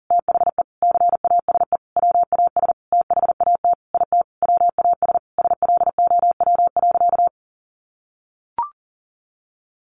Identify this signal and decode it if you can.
Morse code, heard as a keyed tone, about 40 wpm, 700 Hz. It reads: THE CASE WAS THAT IT WAS SLOW.
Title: Sentences from Top 1400 Words 40wpm